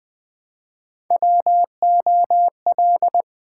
Morse code, heard as a keyed tone, about 20 wpm, 700 Hz. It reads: WOL
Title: Build A Word - 1 Letter At A Time - 4-Letter Words - 20wpm